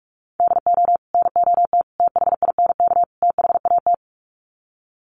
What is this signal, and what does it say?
Morse code, heard as a keyed tone, about 44 wpm, 700 Hz. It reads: DO NOT THINK THAT
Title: Sets of 4 Words 45wpm